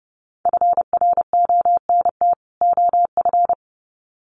Morse code, heard as a keyed tone, about 30 wpm, 700 Hz. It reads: FRONT OF
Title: Sets of 2 Words 30wpm